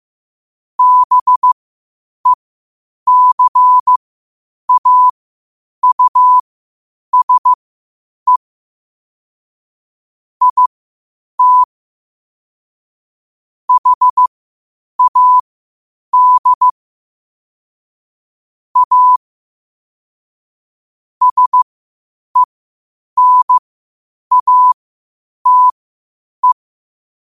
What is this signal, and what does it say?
Morse code, heard as a keyed tone, about 15 wpm, spaced as for 8 wpm, 1000 Hz. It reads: BECAUSE IT HAD A SENATE
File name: qotdfiles / 8wpm